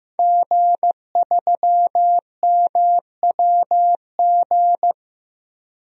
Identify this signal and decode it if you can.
Morse code, heard as a keyed tone, about 16 wpm, 700 Hz. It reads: G3MWG
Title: DXCC-100 Call Signs - 5 Characters - Repeated 3x - 15wpm